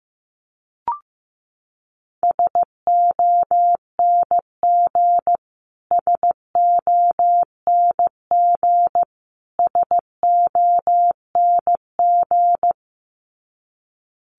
Morse code, heard as a keyed tone, about 15 wpm, 700 Hz. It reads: SONG SONG SONG
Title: Top 500 Words Repeated 15wpm